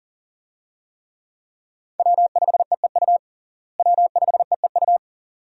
Morse code, heard as a keyed tone, about 40 wpm, 700 Hz. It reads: W5EEU W5EEU
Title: USA Call Signs - 5 Characters - Repeated 2x - 40wpm